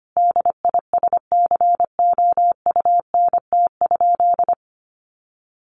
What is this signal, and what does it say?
Morse code, heard as a keyed tone, about 25 wpm, 700 Hz. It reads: DISCOUNT?
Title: Sentences from Top 1200 Words 25wpm